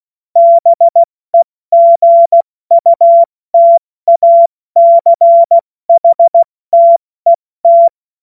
Morse code, heard as a keyed tone, about 16 wpm, 680 Hz. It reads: BEGUTACHTET
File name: Morsekurs / Witze / Witz 06. Dezember 2022 70 F80